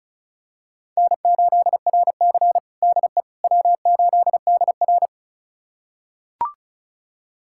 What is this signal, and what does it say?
Morse code, heard as a keyed tone, about 36 wpm, 700 Hz. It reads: N8RC DE W8DR
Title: USA Callsign DE Callsign - 4 Characters - 35wpm